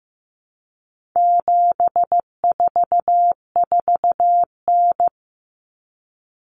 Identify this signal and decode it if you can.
Morse code, heard as a keyed tone, about 15 wpm, 700 Hz. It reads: 744N
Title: Common Cut Numbers - 4-Digit 15wpm